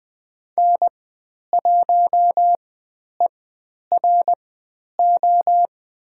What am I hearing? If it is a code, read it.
Morse code, heard as a keyed tone, about 20 wpm, spaced as for 10 wpm, 700 Hz. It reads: N1ERO
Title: 095 Lesson 16g - Letter c - Call Signs 20wpm